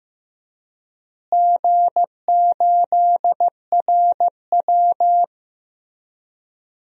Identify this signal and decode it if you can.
Morse code, heard as a keyed tone, about 15 wpm, 700 Hz. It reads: G8RW
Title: DXCC-100 Call Signs - 4 Characters - Repeated 3x - 15wpm